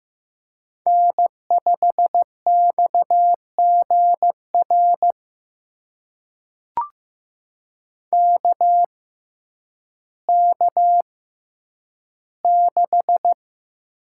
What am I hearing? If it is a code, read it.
Morse code, heard as a keyed tone, about 15 wpm, spaced as for 9 wpm, 700 Hz. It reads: N5XGR KK6
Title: USA Call Signs - 6x Character Spacing - 15wpm